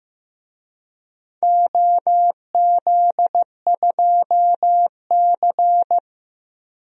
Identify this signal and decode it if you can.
Morse code, heard as a keyed tone, about 15 wpm, 700 Hz. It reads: OZ2C
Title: DXCC-100 Call Signs - 4 Characters - 15wpm